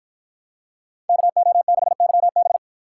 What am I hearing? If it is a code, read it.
Morse code, heard as a keyed tone, about 51 wpm, 700 Hz. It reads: KO6YB